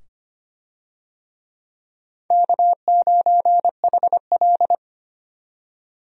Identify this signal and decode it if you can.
Morse code, heard as a keyed tone, about 25 wpm, 700 Hz. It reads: K9HL
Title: USA Call Signs - 4 Characters - 25wpm